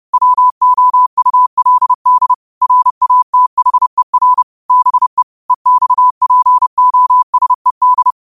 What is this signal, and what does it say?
Morse code, heard as a keyed tone, about 30 wpm, 1000 Hz. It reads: WOULD RATHER BE EXPOSED
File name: qotdfiles / 30wpm